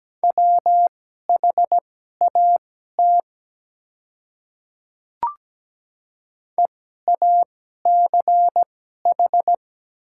Morse code, heard as a keyed tone, about 17 wpm, spaced as for 12 wpm, 700 Hz. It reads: WHAT EACH
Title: 2x Character Spacing - Top 100 Words - 17wpm